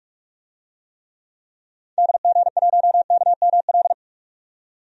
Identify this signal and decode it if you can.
Morse code, heard as a keyed tone, about 41 wpm, 700 Hz. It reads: DG1KML
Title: DXCC-100 Call Signs - 6 Characters - 45wpm